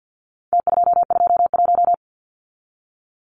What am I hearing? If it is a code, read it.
Morse code, heard as a keyed tone, about 48 wpm, 700 Hz. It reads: T221
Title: Numbers - Contest Style - 4-Digit 50wpm